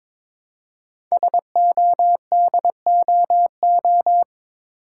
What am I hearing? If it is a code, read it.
Morse code, heard as a keyed tone, about 22 wpm, 700 Hz. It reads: SODOO
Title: Code Groups - 5 Letters - 22wpm